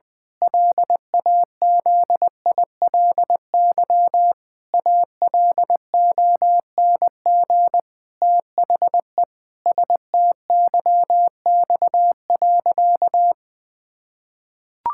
Morse code, heard as a keyed tone, about 20 wpm, 700 Hz. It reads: LAZILY ALONG THE STYX.
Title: Pangram Challenge v2 - 20wpm